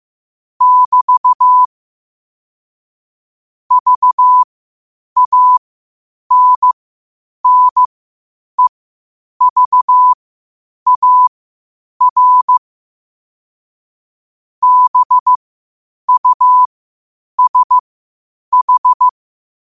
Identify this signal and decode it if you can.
Morse code, heard as a keyed tone, about 15 wpm, spaced as for 8 wpm, 1000 Hz. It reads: <BT> VANNEVAR BUSH
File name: qotdfiles / 8wpm